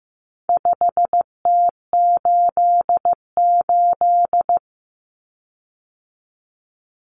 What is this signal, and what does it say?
Morse code, heard as a keyed tone, about 15 wpm, 700 Hz. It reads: 5T88
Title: Common Cut Numbers - 4-Digit 15wpm